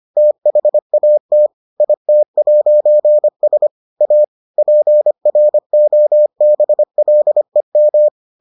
Morse code, heard as a keyed tone, about 25 wpm, 600 Hz. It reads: THAT IT'S A PROBLEM